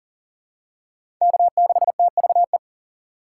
Morse code, heard as a keyed tone, about 37 wpm, 700 Hz. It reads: K6TVE
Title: USA Call Signs - 4x Character Spacing - 40wpm